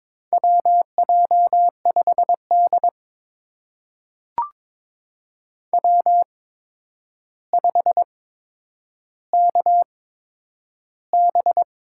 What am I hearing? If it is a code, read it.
Morse code, heard as a keyed tone, about 22 wpm, spaced as for 9 wpm, 700 Hz. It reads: WJ5D W5KB
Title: USA Call Signs - 8x Character Spacing - 22wpm